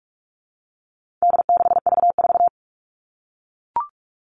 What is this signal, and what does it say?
Morse code, heard as a keyed tone, about 44 wpm, 700 Hz. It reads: D6V4
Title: All Cut Numbers - 4-Digit 45wpm